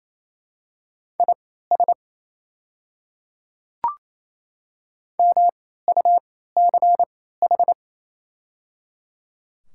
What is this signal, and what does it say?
Morse code, heard as a keyed tone, about 28 wpm, spaced as for 16 wpm, 700 Hz. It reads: IS MUCH
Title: 3x Character Spacing - Top 300 Words - 28wpm